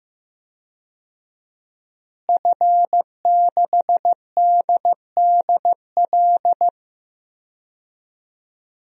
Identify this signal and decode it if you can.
Morse code, heard as a keyed tone, about 15 wpm, 700 Hz. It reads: F6DDL
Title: DXCC-100 Call Signs - 5 Characters - Repeated 3x - 15wpm